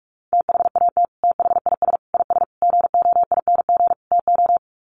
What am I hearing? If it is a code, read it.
Morse code, heard as a keyed tone, about 44 wpm, 700 Hz. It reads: THAT THIS IS GOING TO
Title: Sets of 5 Words Encore2 45wpm